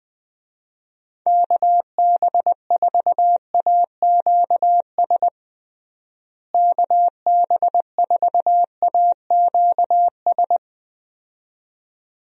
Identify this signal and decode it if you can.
Morse code, heard as a keyed tone, about 20 wpm, 700 Hz. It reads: KB4AQS KB4AQS
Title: USA Call Signs - 6 Characters - Repeated 2x - 20wpm